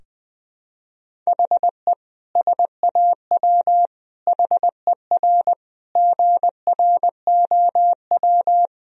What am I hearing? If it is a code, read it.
Morse code, heard as a keyed tone, about 20 wpm, 700 Hz. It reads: HE SAW HER GROW
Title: Sentences from Top 1000 Words - Review Part 2 - 20wpm